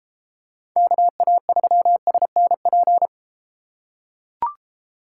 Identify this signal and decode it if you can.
Morse code, heard as a keyed tone, about 30 wpm, 700 Hz. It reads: KA3SNP